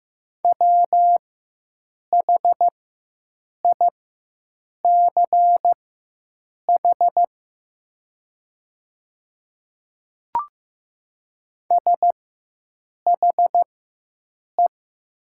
Morse code, heard as a keyed tone, about 15 wpm, spaced as for 7 wpm, 700 Hz. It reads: WHICH SHE
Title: 4x Character Spacing - Top 200 Words - 15wpm